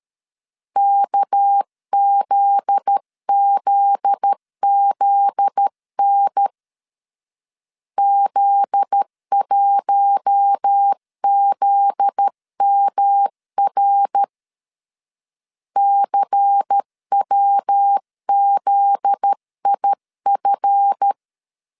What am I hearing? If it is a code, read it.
Morse code, heard as a keyed tone, about 13 wpm, 800 Hz. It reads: KZZZN Z1ZMR CWZIF
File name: LEZIONI / 7 GF9Z / 7 GF9Z 12wpm 2147